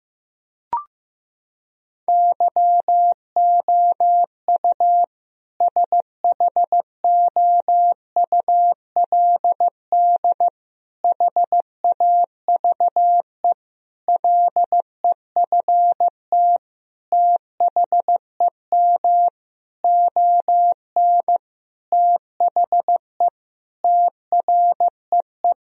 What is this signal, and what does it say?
Morse code, heard as a keyed tone, about 16 wpm, 700 Hz. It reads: YOU SHOULD HAVE LEFT THEM ON THE TREE
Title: Sentences from Top 1000 Words - Review Part 1 - 15wpm